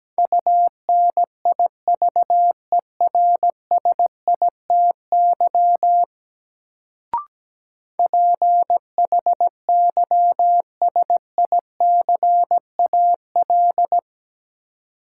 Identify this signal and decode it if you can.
Morse code, heard as a keyed tone, about 17 wpm, 700 Hz. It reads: UNIVERSITY PHYSICAL
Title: Sentences from Top 1000 Words - Review Part 4 - New Word Drill - 17wpm